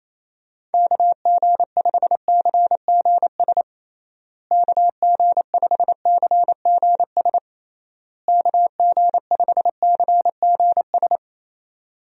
Morse code, heard as a keyed tone, about 28 wpm, 700 Hz. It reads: KG5CGS KG5CGS KG5CGS
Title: USA Call Signs - 6 Characters - Repeated 3x - 28wpm